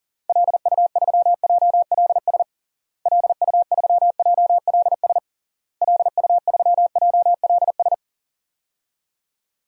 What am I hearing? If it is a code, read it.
Morse code, heard as a keyed tone, about 40 wpm, 700 Hz. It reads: LU3JLS LU3JLS LU3JLS
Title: DXCC-100 Call Signs - 6 Characters - Repeated 3x - 40wpm